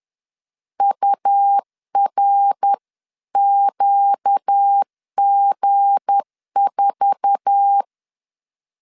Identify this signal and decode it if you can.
Morse code, heard as a keyed tone, about 10 wpm, 800 Hz. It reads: URQG4